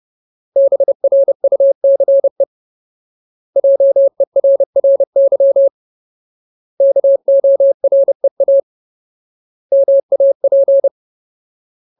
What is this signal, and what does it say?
Morse code, heard as a keyed tone, about 30 wpm, 550 Hz. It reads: BRUCE JERRY KOREA MAP
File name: Ultimate Word List / 30WPM Ultimate